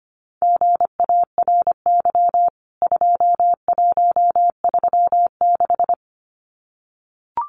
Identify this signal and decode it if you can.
Morse code, heard as a keyed tone, about 25 wpm, 700 Hz. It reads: GARY 2136
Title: CWT Exchange 25wpm